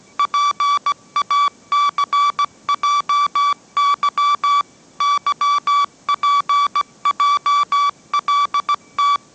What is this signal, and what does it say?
Morse code, heard as a keyed tone, about 17 wpm, 1150 Hz. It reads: PACJY YPJLT
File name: mrx / mrx080 18wpm